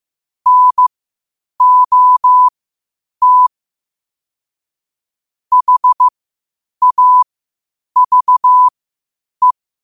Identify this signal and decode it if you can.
Morse code, heard as a keyed tone, about 15 wpm, spaced as for 8 wpm, 1000 Hz. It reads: NOT HAVE
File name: qotdfiles / 8wpm